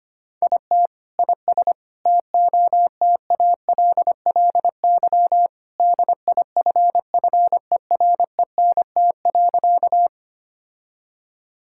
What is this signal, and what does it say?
Morse code, heard as a keyed tone, about 25 wpm, 700 Hz. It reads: IT IS TOTALLY DIFFERENT.
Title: Sentences from Top 1600 Words 25wpm